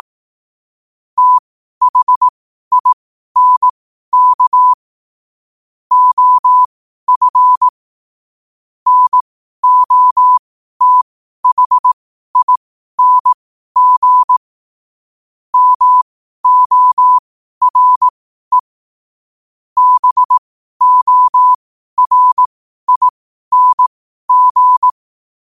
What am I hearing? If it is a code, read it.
Morse code, heard as a keyed tone, about 18 wpm, spaced as for 12 wpm, 1000 Hz. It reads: THINK OF NOTHING MORE BORING
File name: qotdfiles / 12wpm